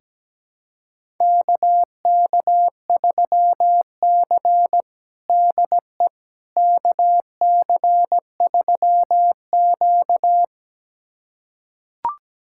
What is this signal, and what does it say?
Morse code, heard as a keyed tone, about 17 wpm, 700 Hz. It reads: KK3C DE KC3Q
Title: USA Callsign DE Callsign - 4 Characters - 17wpm